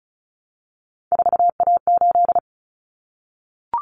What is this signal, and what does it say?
Morse code, heard as a keyed tone, about 34 wpm, 700 Hz. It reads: 4A8
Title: Common Cut Numbers - 3-Digit 35wpm